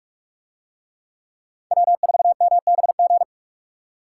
Morse code, heard as a keyed tone, about 41 wpm, 700 Hz. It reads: W4MBG